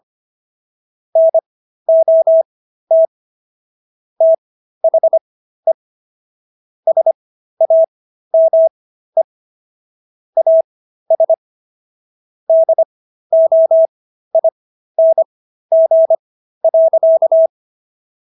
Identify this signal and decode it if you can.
Morse code, heard as a keyed tone, about 25 wpm, spaced as for 13 wpm, 650 Hz. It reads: NOT THE SAME AS DOING.